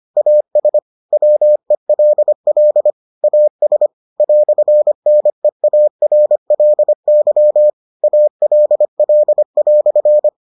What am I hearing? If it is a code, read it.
Morse code, heard as a keyed tone, about 25 wpm, 600 Hz. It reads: AS WELL AS "NEARLY ALL"